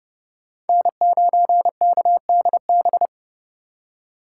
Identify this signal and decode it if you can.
Morse code, heard as a keyed tone, about 30 wpm, 700 Hz. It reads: N9KDB
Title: K1USN SST QSO Practice - 30wpm